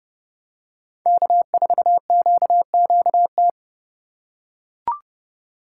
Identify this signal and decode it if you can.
Morse code, heard as a keyed tone, about 30 wpm, 700 Hz. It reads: K4QQT
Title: USA Call Signs - 5 Characters - 4x Character Spacing - 30wpm